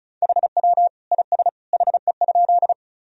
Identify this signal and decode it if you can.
Morse code, heard as a keyed tone, about 34 wpm, 700 Hz. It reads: HW IS HE?
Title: Sentences from Top 100 Words and Abbreviations 35wpm